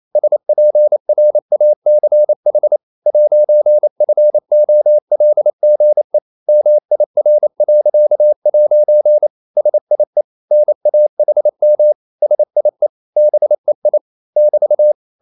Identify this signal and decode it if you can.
Morse code, heard as a keyed tone, about 28 wpm, 600 Hz. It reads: SPRACH 'FOLGE MIR.' SIE NAHM SIE BEI <BT>
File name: marienkind9 28wpm